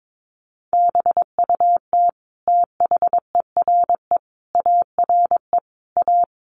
Morse code, heard as a keyed tone, about 22 wpm, 700 Hz. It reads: BUT THERE ARE A
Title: Sets of 5 Words Encore2 22wpm